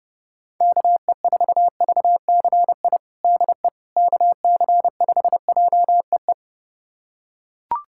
Morse code, heard as a keyed tone, about 30 wpm, 700 Hz. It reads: KE4VCI DE KC5JEE
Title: USA Callsign DE Callsign - 6 Characters - 30wpm